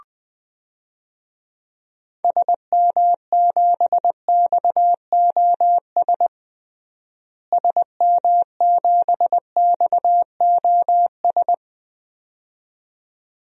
Morse code, heard as a keyed tone, about 20 wpm, 700 Hz. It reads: SM7XOS SM7XOS
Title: DXCC-100 Call Signs - 6 Characters - Repeated 2x - 20wpm